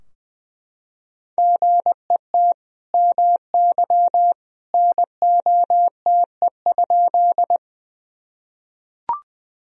Morse code, heard as a keyed tone, about 20 wpm, 700 Hz. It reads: GET MY NOTE?
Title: Sentences from Top 1000 Words - Review Part 2 - 20wpm